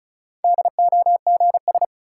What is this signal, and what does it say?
Morse code, heard as a keyed tone, about 34 wpm, 700 Hz. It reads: DOGS
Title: Sentences from Top 1600 Words - New Word Drill Alt - 35wpm